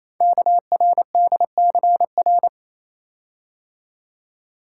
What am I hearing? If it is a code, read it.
Morse code, heard as a keyed tone, about 28 wpm, 700 Hz. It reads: KRDCR
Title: Code Groups - 5 Letters - 28wpm